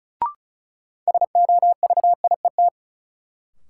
Morse code, heard as a keyed tone, about 36 wpm, 700 Hz. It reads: SOVIET
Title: Sets of 5 Words Encore - New Word Drill - 35wpm